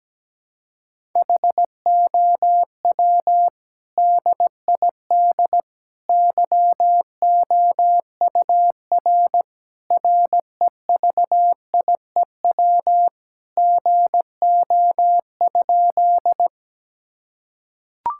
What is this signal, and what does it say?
Morse code, heard as a keyed tone, about 17 wpm, 700 Hz. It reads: HOW DID YOUR REVIEW GO?
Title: Sentences from Top 1000 Words - Review Part 6 - 17wpm